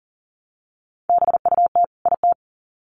Morse code, heard as a keyed tone, about 40 wpm, 700 Hz. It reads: BUT IT
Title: Sets of 2 Words 40wpm